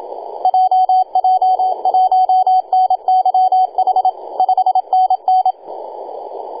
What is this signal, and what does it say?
Morse code, heard as a keyed tone, about 27 wpm, 750 Hz, with radio noise. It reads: JJ1NYH 5NN
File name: e6gg-18-083-cw